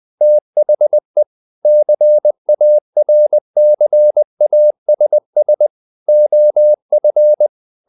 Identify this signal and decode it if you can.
Morse code, heard as a keyed tone, about 20 wpm, 600 Hz. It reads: THE CARCASS OF
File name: the old man and the sea / the old man and the sea part 1 20wpm